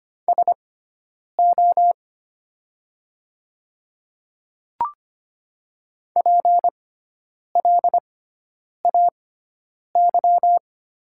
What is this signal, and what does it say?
Morse code, heard as a keyed tone, about 25 wpm, spaced as for 9 wpm, 700 Hz. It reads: SO PLAY
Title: 6x Character Spacing - Top 200 Words - 25wpm